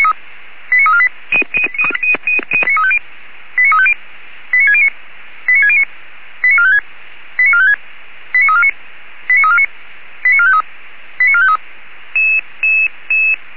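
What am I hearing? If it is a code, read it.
Five-tone selective call (ZVEI), heard as quick runs of five tones, tones of about 70 ms. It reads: 89348 89350 89609 89609 89456 89456 89349 89349 89453 89453
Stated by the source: ZVEI five-tone selective call received on an SDR.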